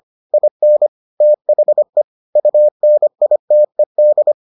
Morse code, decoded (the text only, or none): IN THE UNITED